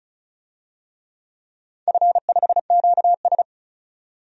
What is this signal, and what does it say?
Morse code, heard as a keyed tone, about 34 wpm, 700 Hz. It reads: F5QS
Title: DXCC-100 Call Signs - 4 Characters - 35wpm